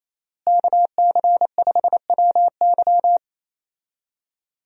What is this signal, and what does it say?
Morse code, heard as a keyed tone, about 27 wpm, 700 Hz. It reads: KC5WY
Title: USA Call Signs - 5 Characters - Repeated 3x - 28wpm